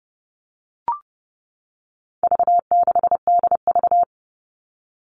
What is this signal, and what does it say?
Morse code, heard as a keyed tone, about 30 wpm, 700 Hz. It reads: V6DV
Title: All Cut Numbers - 4-Digit 30wpm